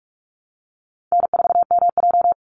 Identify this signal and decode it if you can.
Morse code, heard as a keyed tone, about 46 wpm, 700 Hz. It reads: N4MJ
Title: NA SOTA Chaser Callsigns 45wpm